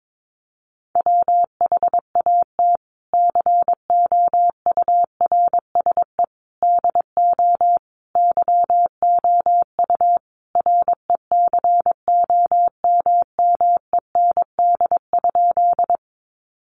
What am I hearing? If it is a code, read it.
Morse code, heard as a keyed tone, about 22 wpm, 700 Hz. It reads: WHAT COURSE DO YOU RECOMMEND?
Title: Sentences from Top 1200 Words 22wpm